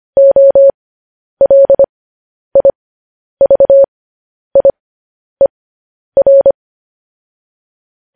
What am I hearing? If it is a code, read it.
Morse code, heard as a keyed tone, about 25 wpm, spaced as for 10 wpm, 550 Hz. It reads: OLIVIER
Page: news headlines in morse code at 10 wpm